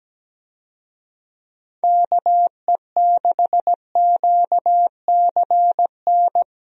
Morse code, heard as a keyed tone, about 17 wpm, 700 Hz. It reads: KE6QCN
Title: USA Call Signs - 6 Characters - Repeated 2x - 17wpm